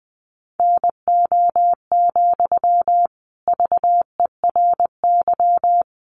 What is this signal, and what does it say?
Morse code, heard as a keyed tone, about 20 wpm, 700 Hz. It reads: NO, VERY
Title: Sentences from Top 1500 Words 20wpm